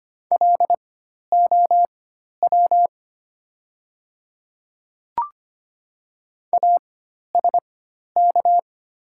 Morse code, heard as a keyed tone, about 25 wpm, spaced as for 12 wpm, 700 Hz. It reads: LOW ASK
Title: 4x Character Spacing - Top 200 Words - 25wpm